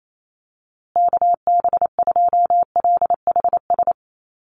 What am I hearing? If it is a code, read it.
Morse code, heard as a keyed tone, about 28 wpm, 700 Hz. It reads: KB2LHS
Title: USA Call Signs Repeated 3x 28wpm